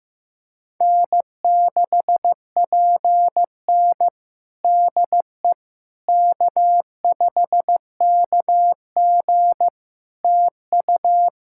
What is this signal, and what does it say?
Morse code, heard as a keyed tone, about 15 wpm, 700 Hz. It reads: N6PN DE K5KG TU
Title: Practice QSO - Exchange 4 - Ending - 15wpm